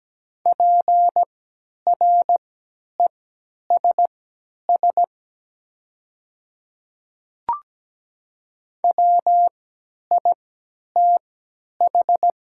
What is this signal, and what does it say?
Morse code, heard as a keyed tone, about 17 wpm, spaced as for 10 wpm, 700 Hz. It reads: PRESS WITH
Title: 3x Character Spacing - Top 500 Words - 17wpm